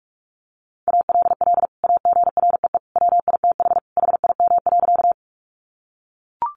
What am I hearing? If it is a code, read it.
Morse code, heard as a keyed tone, about 44 wpm, 700 Hz. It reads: ALL AGREE WITH HIM.
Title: Sentences from Top 1100 Words 45wpm